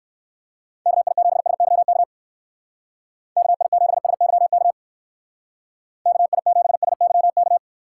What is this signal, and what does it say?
Morse code, heard as a keyed tone, about 56 wpm, 700 Hz. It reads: KI7SYK KI7SYK KI7SYK
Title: Speed Racing - USA 2x3 Call Signs 45wpm